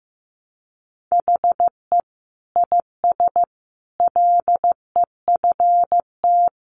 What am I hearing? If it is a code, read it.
Morse code, heard as a keyed tone, about 15 wpm, 700 Hz. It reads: HE IS LEFT
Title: Sentences from Top 1100 Words 15wpm